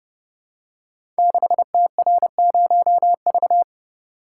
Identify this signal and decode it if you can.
Morse code, heard as a keyed tone, about 28 wpm, 700 Hz. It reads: 6TR0V
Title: Code Groups - 5 Letter:Numbers - 30wpm